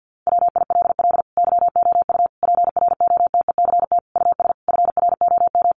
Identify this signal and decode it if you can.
Morse code, heard as a keyed tone, about 50 wpm, 700 Hz. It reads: WILL YOU PROTECT US FROM
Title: Sentences from Top 900 Words 50wpm